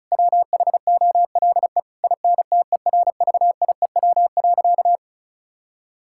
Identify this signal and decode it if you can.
Morse code, heard as a keyed tone, about 36 wpm, 700 Hz. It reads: WHOLE INTERVIEW.
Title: Sentences from Top 1600 Words 35wpm